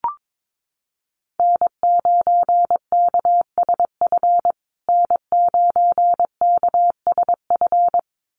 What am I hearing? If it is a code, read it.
Morse code, heard as a keyed tone, about 22 wpm, 700 Hz. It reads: N9KSF N9KSF
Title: USA Call Signs Repeated 2x 22wpm